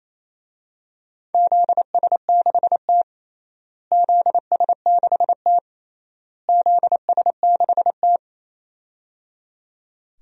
Morse code, heard as a keyed tone, about 28 wpm, 700 Hz. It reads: ZS6T ZS6T ZS6T
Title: DXCC-100 Call Signs - 4 Characters - Repeated 3x - 28wpm